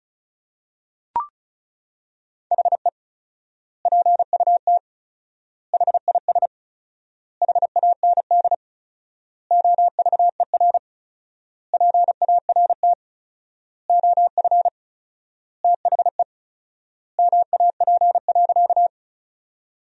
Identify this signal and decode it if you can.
Morse code, heard as a keyed tone, about 36 wpm, 700 Hz. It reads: HE PUT HIS HAND OVER PART OF THE MAP.
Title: 4x Word-Spacing - Sentences from Top 400 Words - 35wpm